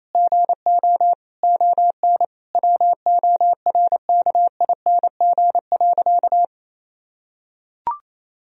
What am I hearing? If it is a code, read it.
Morse code, heard as a keyed tone, about 28 wpm, 700 Hz. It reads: GO ON WORKING.
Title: Sentences from Top 1000 Words - Review Part 4 - 28wpm